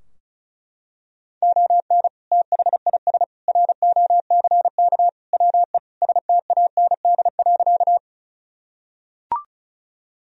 Morse code, heard as a keyed tone, about 34 wpm, 700 Hz. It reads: ON THIS ROCK WE STAND.
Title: Sentences from Top 1000 Words - Review Part 2 - 35wpm